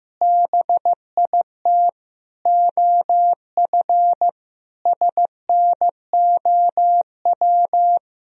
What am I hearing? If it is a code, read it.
Morse code, heard as a keyed tone, about 15 wpm, 700 Hz. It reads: BIT OF SNOW
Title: Sentences from Top 1000 Words - Review Part 3 - 15wpm